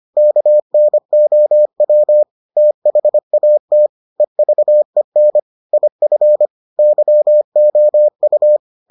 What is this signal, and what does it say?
Morse code, heard as a keyed tone, about 25 wpm, 600 Hz. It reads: KNOW THAT EVEN IF YOU